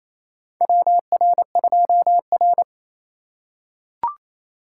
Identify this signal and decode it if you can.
Morse code, heard as a keyed tone, about 28 wpm, 700 Hz. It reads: WR2R
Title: USA Call Signs - 4 Characters - 2x Character Spacing - 28wpm